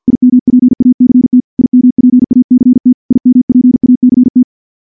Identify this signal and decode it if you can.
Morse code, heard as a keyed tone, about 48 wpm, 250 Hz. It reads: IMPACT IMPACT IMPACT
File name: morse / morse-training-50WPM